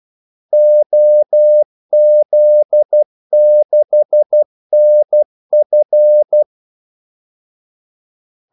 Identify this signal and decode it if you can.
Morse code, heard as a keyed tone, about 12 wpm, 600 Hz. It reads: OZ6NF
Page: Kaldesignaler 12 wpm